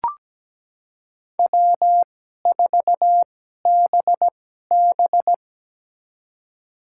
Morse code, heard as a keyed tone, about 17 wpm, spaced as for 12 wpm, 700 Hz. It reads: W4BB
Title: USA Call Signs - 4 Characters - 2x Character Spacing - 17wpm